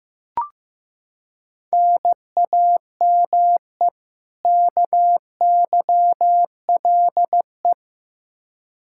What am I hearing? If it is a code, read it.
Morse code, heard as a keyed tone, about 15 wpm, 700 Hz. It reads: NAME KYLE
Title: Practice QSO - Exchange 1 - Beginning - 15wpm